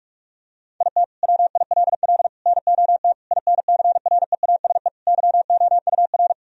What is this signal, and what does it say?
Morse code, heard as a keyed tone, about 46 wpm, 700 Hz. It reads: IT WILL NOT INCREASE YOUR
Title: Sentences from Top 1000 Words - Review Part 4 - 45wpm